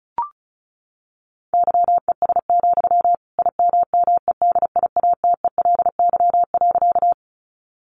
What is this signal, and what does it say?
Morse code, heard as a keyed tone, about 34 wpm, 700 Hz. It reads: YES, IMMEDIATELY.
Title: Sentences from Top 1100 Words 35wpm